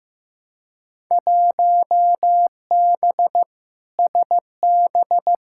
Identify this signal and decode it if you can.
Morse code, heard as a keyed tone, about 15 wpm, 700 Hz. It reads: 1B SB
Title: Field Day Exchanges - 15wpm